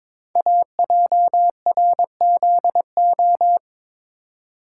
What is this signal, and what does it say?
Morse code, heard as a keyed tone, about 22 wpm, 700 Hz. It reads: AJRZO
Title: Code Groups - 5 Letter:Numbers - 22wpm